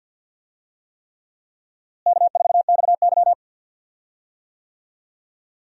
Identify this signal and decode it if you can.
Morse code, heard as a keyed tone, about 51 wpm, 700 Hz. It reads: K4XY
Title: USA Call Signs - 4 Characters - Repeated 3x - 50wpm